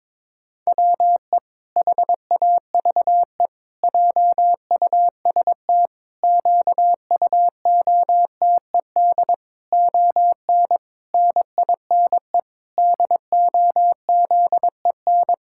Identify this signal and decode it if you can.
Morse code, heard as a keyed tone, about 22 wpm, 700 Hz. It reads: WE HAVE JUST QUOTED ON NINE DOZEN
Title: Pangram Challenge v2 - 22wpm